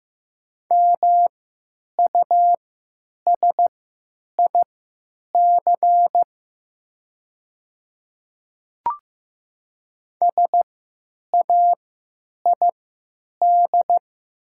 Morse code, heard as a keyed tone, about 15 wpm, spaced as for 8 wpm, 700 Hz. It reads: MUSIC SAID
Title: 3x Character Spacing - Top 500 Words - 15wpm